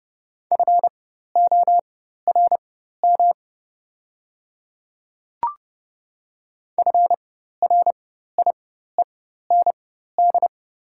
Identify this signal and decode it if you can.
Morse code, heard as a keyed tone, about 31 wpm, spaced as for 14 wpm, 700 Hz. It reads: FORM FRIEND
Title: 4x Character Spacing - Top 300 Words - 30wpm